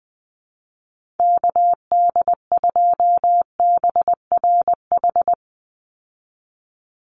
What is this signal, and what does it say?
Morse code, heard as a keyed tone, about 19 wpm, 700 Hz. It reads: KD2BRH
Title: USA Call Signs 20wpm